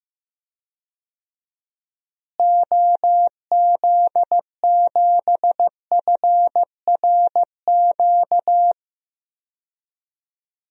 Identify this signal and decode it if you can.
Morse code, heard as a keyed tone, about 15 wpm, 700 Hz. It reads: OZ7FRQ
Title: DXCC-100 Call Signs - 6 Characters - Repeated 3x - 15wpm